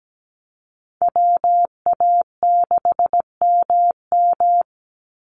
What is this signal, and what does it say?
Morse code, heard as a keyed tone, about 17 wpm, 700 Hz. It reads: WA6MM
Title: NA SOTA Activator Callsigns 17wpm